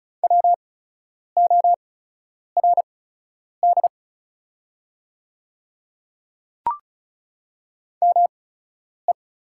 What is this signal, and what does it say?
Morse code, heard as a keyed tone, about 36 wpm, spaced as for 10 wpm, 690 Hz. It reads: WORD ME
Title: 8x Character Spacing - Top 200 Words - 35wpm